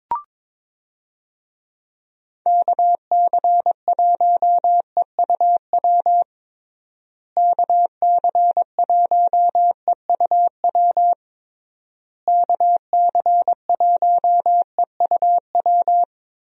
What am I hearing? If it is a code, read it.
Morse code, heard as a keyed tone, about 22 wpm, 700 Hz. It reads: KC1EUW KC1EUW KC1EUW
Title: USA Call Signs - 6 Characters - Repeated 3x - 22wpm